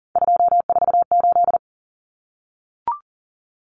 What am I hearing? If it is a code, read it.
Morse code, heard as a keyed tone, about 40 wpm, 700 Hz. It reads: 248
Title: Numbers - 3-Digit 40wpm